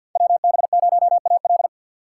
Morse code, heard as a keyed tone, about 49 wpm, 700 Hz. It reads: WB0AL
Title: USA Call Signs - 3x Character Spacing - 50wpm